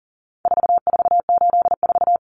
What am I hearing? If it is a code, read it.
Morse code, heard as a keyed tone, about 40 wpm, 700 Hz. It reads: 4484
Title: Common Cut Numbers - 4-Digit 40wpm